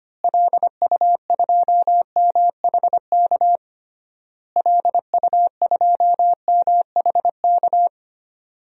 Morse code, heard as a keyed tone, about 25 wpm, 700 Hz. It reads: LU2MHK LU2MHK
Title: DXCC-100 Call Signs - 6 Characters - Repeated 3x - 25wpm